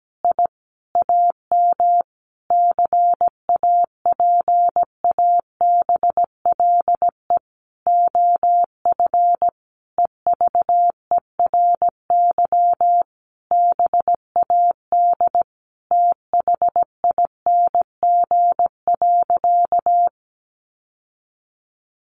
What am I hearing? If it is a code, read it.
Morse code, heard as a keyed tone, about 17 wpm, 700 Hz. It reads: I AM CAPABLE OF EVERY BAD THING.
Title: Sentences from Top 1200 Words 17wpm